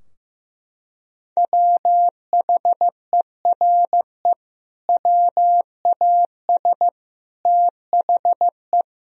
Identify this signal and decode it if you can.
Morse code, heard as a keyed tone, about 16 wpm, 700 Hz. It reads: WHERE WAS THE
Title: Sentences from Top 1000 Words - Review Part 4 - 15wpm